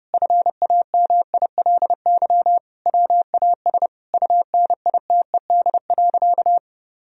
Morse code, heard as a keyed tone, about 32 wpm, 700 Hz. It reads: FAMILY WAS UNITED.
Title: Sentences from Top 1000 Words - Review Part 4 - 30wpm